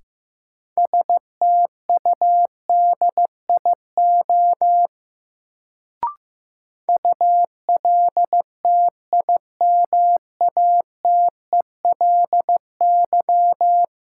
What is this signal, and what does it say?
Morse code, heard as a keyed tone, about 15 wpm, 700 Hz. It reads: STUDIO ULTIMATELY
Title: Sentences from Top 1200 Words - New Word Drill - 15wpm